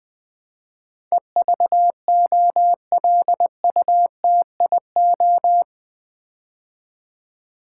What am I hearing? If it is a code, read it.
Morse code, heard as a keyed tone, about 20 wpm, 700 Hz. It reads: EVOLUTIO
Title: Build A Word - 4 Letters At A Time - 12-Letter Words - 20wpm